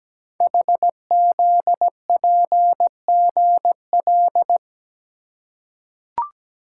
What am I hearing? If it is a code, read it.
Morse code, heard as a keyed tone, about 17 wpm, 700 Hz. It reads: HZPGL
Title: Code Groups - 5 Letters - 17wpm